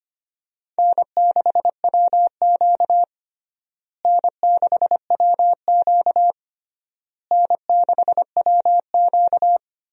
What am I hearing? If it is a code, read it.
Morse code, heard as a keyed tone, about 25 wpm, 700 Hz. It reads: N6WQ N6WQ N6WQ
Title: USA Call Signs - 4 Characters - Repeated 3x - 25wpm